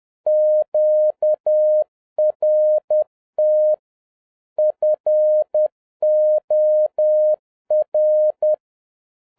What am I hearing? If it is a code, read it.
Morse code, heard as a keyed tone, about 10 wpm, 610 Hz. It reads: QRT FOR